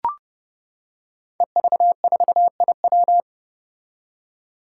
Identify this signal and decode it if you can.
Morse code, heard as a keyed tone, about 30 wpm, 700 Hz. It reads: EV4IW
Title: Five Challenging Letter-Number Combinations 30wpm